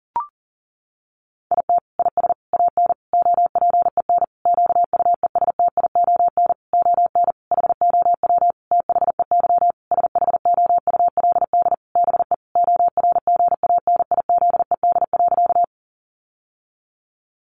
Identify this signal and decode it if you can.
Morse code, heard as a keyed tone, about 37 wpm, 700 Hz. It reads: IT IS AN OPEN QUESTION ON HOW THEY SHOULD BE ORGANIZED.
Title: Sentences from Top 1300 Words 40wpm